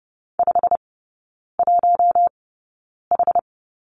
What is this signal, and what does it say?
Morse code, heard as a keyed tone, about 30 wpm, spaced as for 9 wpm, 700 Hz. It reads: 51H
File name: course / 30:10wpm / 087 Lesson 15e - Number 5 - Three Characters 30wpm